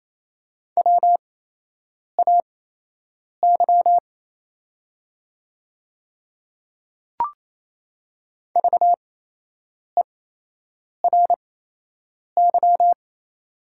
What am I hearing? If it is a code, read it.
Morse code, heard as a keyed tone, about 28 wpm, spaced as for 8 wpm, 700 Hz. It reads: WAY VERY
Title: 8x Character Spacing - Top 300 Words - 28wpm